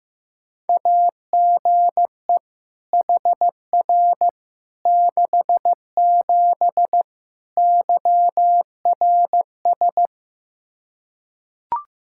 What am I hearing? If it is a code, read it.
Morse code, heard as a keyed tone, about 15 wpm, 700 Hz. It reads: AGE HR 67 YRS
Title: K1USN SST QSO Practice 15wpm